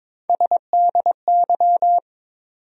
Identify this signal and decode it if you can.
Morse code, heard as a keyed tone, about 22 wpm, 700 Hz. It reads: SDY
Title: USA Call Sign Postfixes - 3 Letters 22wpm